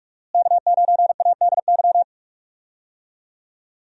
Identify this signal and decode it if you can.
Morse code, heard as a keyed tone, about 41 wpm, 700 Hz. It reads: K9ADY
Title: USA Call Signs - 5 Characters - Repeated 3x - 45wpm